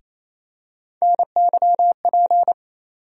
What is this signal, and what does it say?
Morse code, heard as a keyed tone, about 28 wpm, 700 Hz. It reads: NYP
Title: USA Call Sign Postfixes - 3 Letters 28wpm